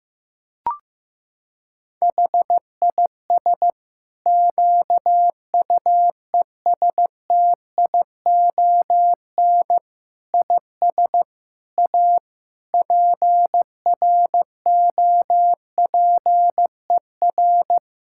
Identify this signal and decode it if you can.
Morse code, heard as a keyed tone, about 15 wpm, 700 Hz. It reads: HIS QUESTION IS A PROPER
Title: Sentences from Top 1000 Words - Review Part 5 - 15wpm